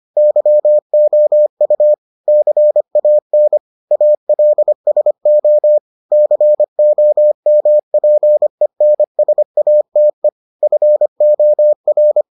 Morse code, heard as a keyed tone, about 25 wpm, 600 Hz. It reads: YOU CAN ALSO COMPENSATE FOR